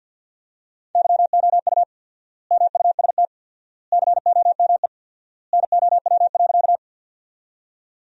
Morse code, heard as a keyed tone, about 50 wpm, 700 Hz. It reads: YOU MUST COME NOW.
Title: 4x Word-Spacing - Sentences from Top 200 Words - 50wpm